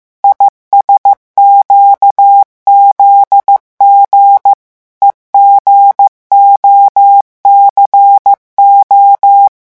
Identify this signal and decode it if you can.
Morse code, heard as a keyed tone, about 15 wpm, 800 Hz. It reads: ISQZG EGOCO
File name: morse mp3 / revision12-vitesse 14 mots